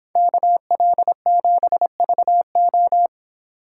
Morse code, heard as a keyed tone, about 25 wpm, 700 Hz. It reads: KL7VO
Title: Speed Racing - USA 2x2 Call Signs 22wpm